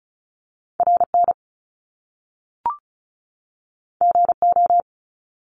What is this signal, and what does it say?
Morse code, heard as a keyed tone, about 36 wpm, 700 Hz. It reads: RN GO
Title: Top Two Letter Combinations in Words 35wpm